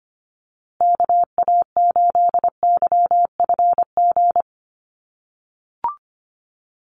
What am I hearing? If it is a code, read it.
Morse code, heard as a keyed tone, about 25 wpm, 700 Hz. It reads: KA8YFG
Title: USA Call Signs Repeated 2x 25wpm